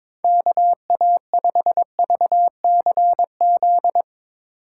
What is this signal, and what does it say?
Morse code, heard as a keyed tone, about 22 wpm, 700 Hz. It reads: KA5VCZ
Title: USA Call Signs - 6 Characters - 2x Character Spacing - 22wpm